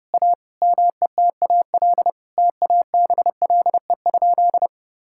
Morse code, heard as a keyed tone, about 32 wpm, 700 Hz. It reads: A METAL TABLE?
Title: Sentences from Top 1000 Words - Review Part 6 - 30wpm